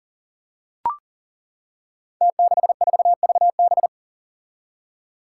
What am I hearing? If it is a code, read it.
Morse code, heard as a keyed tone, about 39 wpm, 700 Hz. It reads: T64VB
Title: Five Challenging Letter-Number Combinations 40wpm